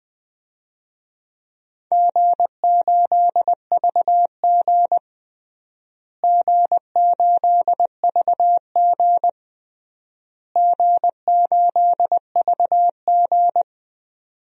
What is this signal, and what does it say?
Morse code, heard as a keyed tone, about 21 wpm, 700 Hz. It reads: G8VG G8VG G8VG
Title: DXCC-100 Call Signs - 4 Characters - Repeated 3x - 20wpm